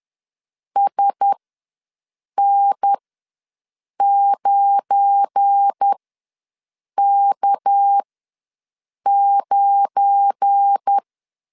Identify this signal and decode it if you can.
Morse code, heard as a keyed tone, about 10 wpm, spaced as for 6 wpm, 800 Hz. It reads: SN9K9
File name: LEZIONI / 7 GF9Z / 7 GF9 ws 10wpm 4034